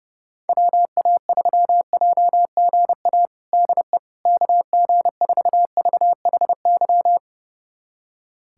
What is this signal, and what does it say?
Morse code, heard as a keyed tone, about 30 wpm, 690 Hz. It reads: WA3JGA DE KG4VHY
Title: USA Callsign DE Callsign 30wpm